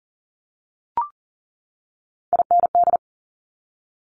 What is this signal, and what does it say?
Morse code, heard as a keyed tone, about 38 wpm, 700 Hz. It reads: IND
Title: Multi-letter Phonemes 40wpm